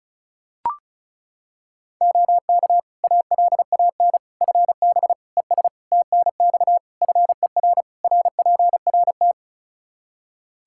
Morse code, heard as a keyed tone, about 36 wpm, 700 Hz. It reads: OK ALAN FB ES TNX FER RPRT
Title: Practice QSO - Exchange 2 - Info - 35wpm